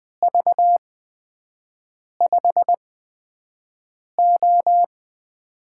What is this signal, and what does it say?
Morse code, heard as a keyed tone, about 20 wpm, spaced as for 5 wpm, 700 Hz. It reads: V5O
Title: USA Call Signs - 8x Character Spacing - 20wpm